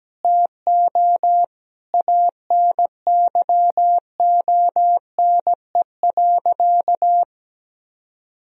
Morse code, heard as a keyed tone, about 17 wpm, 700 Hz. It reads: TO ANYONE.
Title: Sentences from Top 1000 Words - Review Part 2 - 17wpm